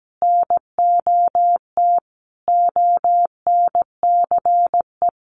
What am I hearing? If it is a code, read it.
Morse code, heard as a keyed tone, about 17 wpm, 700 Hz. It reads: NOT ONCE
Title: Sentences from Top 1500 Words 17wpm